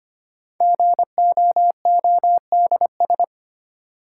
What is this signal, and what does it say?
Morse code, heard as a keyed tone, about 25 wpm, 700 Hz. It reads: GOODS
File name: 5 Letter Words - First 3 Letters Given - 25wpm